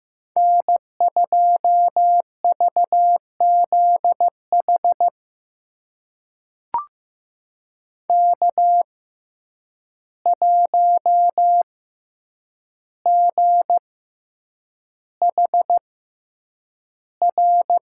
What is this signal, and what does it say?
Morse code, heard as a keyed tone, about 15 wpm, spaced as for 8 wpm, 700 Hz. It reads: N2VZH K1GHR
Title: USA Call Signs - 5 Characters - 6x Character Spacing - 15wpm